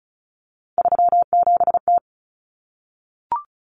Morse code, heard as a keyed tone, about 34 wpm, 700 Hz. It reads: 37T
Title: Common Cut Numbers - 3-Digit 35wpm